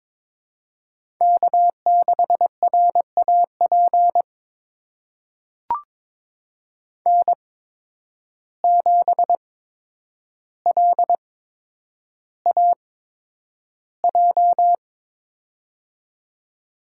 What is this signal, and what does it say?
Morse code, heard as a keyed tone, about 22 wpm, spaced as for 9 wpm, 700 Hz. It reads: K6RAP N7LAJ
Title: USA Call Signs - 5 Characters - 8x Character Spacing - 22wpm